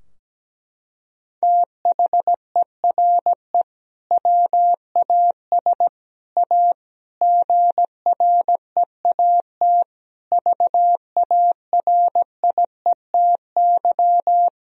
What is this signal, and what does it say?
Morse code, heard as a keyed tone, about 17 wpm, 700 Hz. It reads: THERE WAS A GREAT VARIETY
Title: Sentences from Top 1000 Words - Review Part 4 - 17wpm